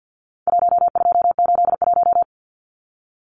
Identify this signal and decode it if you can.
Morse code, heard as a keyed tone, about 52 wpm, 700 Hz. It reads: 1281
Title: Numbers - 4-Digit 50wpm